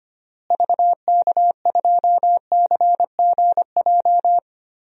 Morse code, heard as a keyed tone, about 25 wpm, 700 Hz. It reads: VK2CGJ
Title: DXCC-100 Call Signs - 6 Characters - Repeated 3x - 25wpm